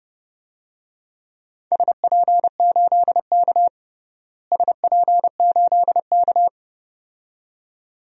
Morse code, heard as a keyed tone, about 32 wpm, 700 Hz. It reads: SP8K SP8K
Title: DXCC-100 Call Signs - 4 Characters - Repeated 2x - 30wpm